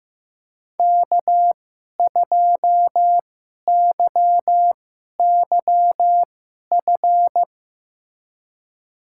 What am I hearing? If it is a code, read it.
Morse code, heard as a keyed tone, about 15 wpm, spaced as for 11 wpm, 700 Hz. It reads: K2YYF
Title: USA Call Signs - 5 Characters - 2x Character Spacing - 15wpm